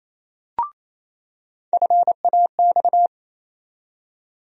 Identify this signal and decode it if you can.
Morse code, heard as a keyed tone, about 28 wpm, 700 Hz. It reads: FAX
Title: USA Call Sign Postfixes - 3 Letters 28wpm